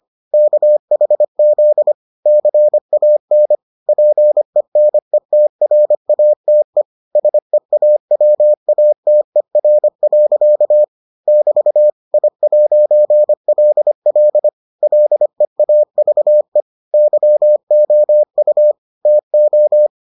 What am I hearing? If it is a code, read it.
Morse code, heard as a keyed tone, about 25 wpm, 600 Hz. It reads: KHZ CAN PENETRATE SEAWATER. <BT> I'LL LEAVE YOU TO